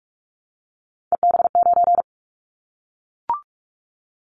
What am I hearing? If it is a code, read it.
Morse code, heard as a keyed tone, about 44 wpm, 700 Hz. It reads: EB9
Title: Callsign Prefixes for Top 100 DXCC - 45wpm